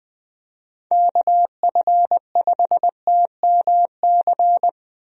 Morse code, heard as a keyed tone, about 19 wpm, 700 Hz. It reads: KF5TMC
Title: USA Call Signs - 6 Characters - Repeated 3x - 20wpm